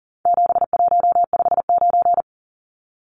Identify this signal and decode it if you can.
Morse code, heard as a keyed tone, about 40 wpm, 700 Hz. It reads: 7159
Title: Numbers - 4-Digit 40wpm